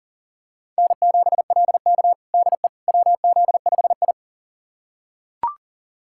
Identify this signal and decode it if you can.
Morse code, heard as a keyed tone, about 40 wpm, 700 Hz. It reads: N7LK DE WZ5I
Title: USA Callsign DE Callsign - 4 Characters - 40wpm